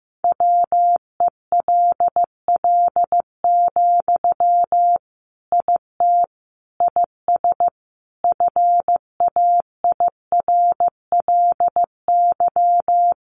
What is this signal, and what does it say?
Morse code, heard as a keyed tone, about 15 wpm, 700 Hz. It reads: WELL, IT IS FAIRLY
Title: Sentences from Top 1100 Words 15wpm